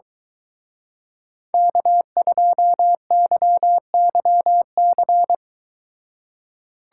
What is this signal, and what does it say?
Morse code, heard as a keyed tone, about 23 wpm, 700 Hz. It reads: K2YYC